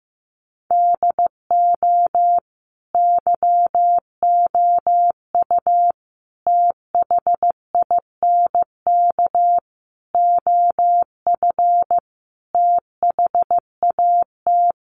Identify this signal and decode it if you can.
Morse code, heard as a keyed tone, about 15 wpm, 700 Hz. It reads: DO YOU THINK OF THAT
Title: Sets of 5 Words Encore2 15wpm